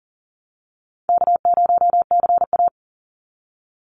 Morse code, heard as a keyed tone, about 40 wpm, 700 Hz. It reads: K0CA
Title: Call Signs Repeated 2x - 40wpm